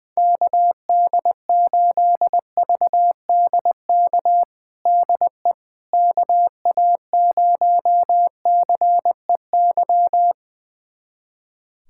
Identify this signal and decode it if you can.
Morse code, heard as a keyed tone, about 20 wpm, 700 Hz. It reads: KD8VDK DE KA0CEY
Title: USA Callsign DE Callsign 20wpm